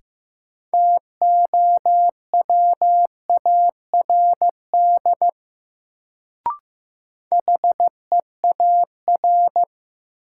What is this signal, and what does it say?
Morse code, heard as a keyed tone, about 16 wpm, 700 Hz. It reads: TOWARD HEAR
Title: Sets of 5 Words - New Word Drill - 15wpm